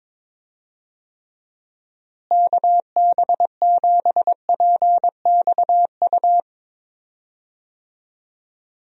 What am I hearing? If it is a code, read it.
Morse code, heard as a keyed tone, about 22 wpm, 700 Hz. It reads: KB7PXU
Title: USA Call Signs - 6 Characters - Repeated 3x - 22wpm